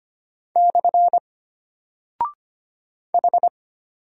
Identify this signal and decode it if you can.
Morse code, heard as a keyed tone, about 25 wpm, 700 Hz. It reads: / H
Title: Single Character (Letter:Number:Punctuation:Prosign) Alt 25wpm